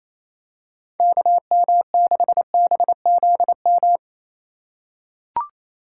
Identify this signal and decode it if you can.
Morse code, heard as a keyed tone, about 28 wpm, 700 Hz. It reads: KM6BZM
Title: USA Call Signs - 6 Characters - 3x Character Spacing - 28wpm